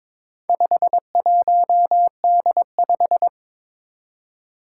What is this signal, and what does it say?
Morse code, heard as a keyed tone, about 22 wpm, 700 Hz. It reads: 51D5
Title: Four Challenging Letter-Number Combinations 22wpm